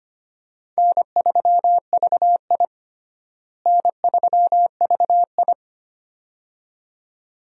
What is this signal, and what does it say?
Morse code, heard as a keyed tone, about 25 wpm, 700 Hz. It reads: N3VI N3VI
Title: USA Call Signs - 4 Characters - Repeated 2x - 25wpm